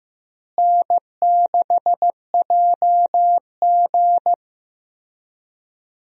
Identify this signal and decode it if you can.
Morse code, heard as a keyed tone, about 15 wpm, 700 Hz. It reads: N6JG
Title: USA Call Signs - 4 Characters - Repeated 2x - 15wpm